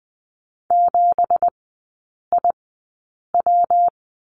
Morse code, heard as a keyed tone, about 20 wpm, spaced as for 8 wpm, 700 Hz. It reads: 7IW
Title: 166 Lesson 27e - Number 7 - Three Characters 20wpm